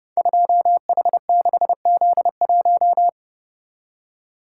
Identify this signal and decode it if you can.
Morse code, heard as a keyed tone, about 29 wpm, 700 Hz. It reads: 2H6Z1
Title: Code Groups - 5 Letter:Numbers - 30wpm